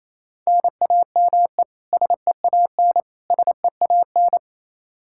Morse code, heard as a keyed tone, about 28 wpm, 700 Hz. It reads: NAME SEAN SEAN
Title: Practice QSO - Exchange 1 - Beginning - 28wpm